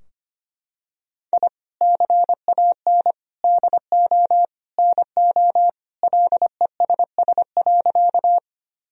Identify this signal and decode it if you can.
Morse code, heard as a keyed tone, about 25 wpm, 700 Hz. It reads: I CAN DO NO LESS.
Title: Sentences from Top 1000 Words - Review Part 2 - 25wpm